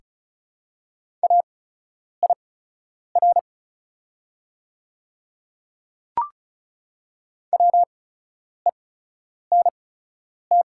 Morse code, heard as a keyed tone, about 36 wpm, spaced as for 10 wpm, 700 Hz. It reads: AIR WENT
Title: 8x Character Spacing - Top 200 Words - 35wpm